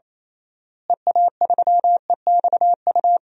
Morse code, heard as a keyed tone, about 28 wpm, 700 Hz. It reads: EA3EXU
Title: DXCC-100 Call Signs - 6 Characters - Repeated 2x - 28wpm